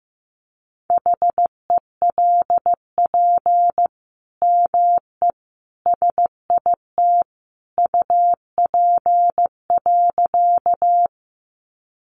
Morse code, heard as a keyed tone, about 15 wpm, 700 Hz. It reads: HELP ME SIT UP.
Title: Sentences from Top 500 Words 15wpm